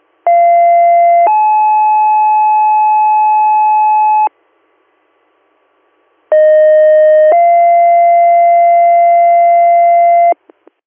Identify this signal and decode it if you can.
Two-tone page, heard as a one-second tone, then a three-second tone, about 690 then 860 Hz, about 620 then 690 Hz. It is heard from 0.3 to 10.3 s.